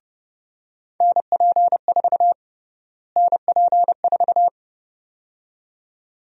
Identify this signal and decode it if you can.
Morse code, heard as a keyed tone, about 30 wpm, 700 Hz. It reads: NP4 NP4
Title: USA Call Sign Prefixes - Repeated 2x - 30wpm